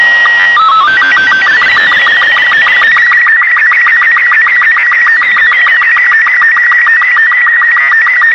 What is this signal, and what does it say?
SSTV (slow-scan television), Robot 36.